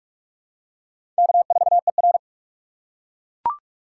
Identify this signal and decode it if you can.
Morse code, heard as a keyed tone, about 44 wpm, 700 Hz. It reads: K4ER
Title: USA Call Signs - 4 Characters - 6x Character Spacing - 45wpm